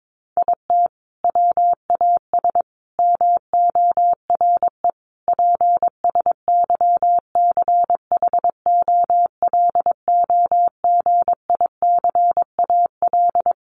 Morse code, heard as a keyed tone, about 22 wpm, 700 Hz. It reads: IT WAS MORE PSYCHOLOGICAL
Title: Sentences from Top 1400 Words 22wpm